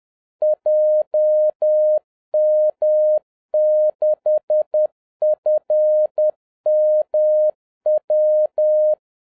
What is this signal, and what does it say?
Morse code, heard as a keyed tone, about 10 wpm, 610 Hz. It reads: JM6FMW
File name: 2011-10-mp3 / n11-MEETING WITH GOOD FRIENDS 10wpm